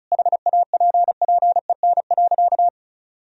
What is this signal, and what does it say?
Morse code, heard as a keyed tone, about 34 wpm, 700 Hz. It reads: HAPPEN.
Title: Sentences from Top 1000 Words - Review Part 4 - 35wpm